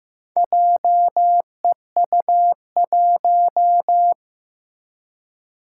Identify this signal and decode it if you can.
Morse code, heard as a keyed tone, about 15 wpm, 700 Hz. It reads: JEU1
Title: Four Challenging Letter-Number Combinations 15wpm